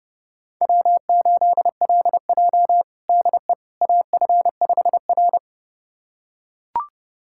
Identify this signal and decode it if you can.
Morse code, heard as a keyed tone, about 30 wpm, 700 Hz. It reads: W8LJ DE AF5R
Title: USA Callsign DE Callsign - 4 Characters - 30wpm